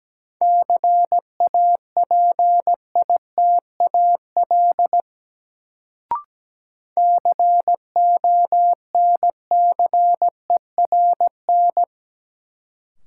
Morse code, heard as a keyed tone, about 17 wpm, 700 Hz. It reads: CAPITAL CONCERN
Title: Sentences from Top 1000 Words - New Word Drill - 17wpm